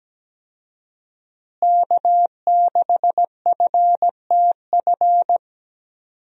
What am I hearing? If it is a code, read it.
Morse code, heard as a keyed tone, about 17 wpm, 700 Hz. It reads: K6FTF
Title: USA Call Signs - 5 Characters - Repeated 3x - 17wpm